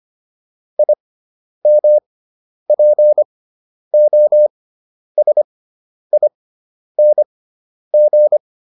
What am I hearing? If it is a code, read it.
Morse code, heard as a keyed tone, about 25 wpm, spaced as for 10 wpm, 600 Hz. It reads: IMPOSING